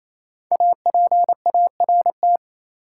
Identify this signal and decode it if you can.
Morse code, heard as a keyed tone, about 28 wpm, 700 Hz. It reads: APART
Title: Five Letter Words - 28wpm